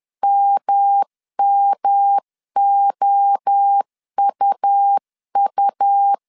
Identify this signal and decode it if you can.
Morse code, heard as a keyed tone, about 10 wpm, 800 Hz. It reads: MMOUU
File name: 10wpm